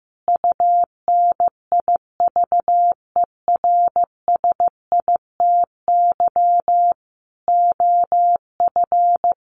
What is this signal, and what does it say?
Morse code, heard as a keyed tone, about 15 wpm, 700 Hz. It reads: UNIVERSITY OF
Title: Sets of 5 Words Encore2 15wpm